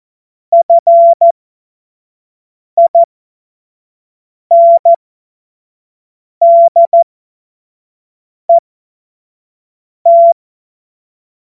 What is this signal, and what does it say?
Morse code, heard as a keyed tone, about 14 wpm, spaced as for 5 wpm, 680 Hz. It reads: FINDET
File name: Witz-19-August-2025-25-F70